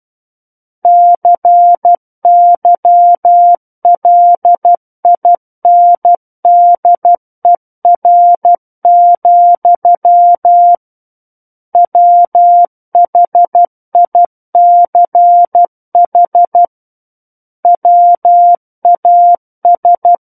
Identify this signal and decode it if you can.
Morse code, heard as a keyed tone, about 12 wpm, 700 Hz. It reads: CYLINDER, WHICH WAS
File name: Book One / 03-Chapter 3 - 12 WPM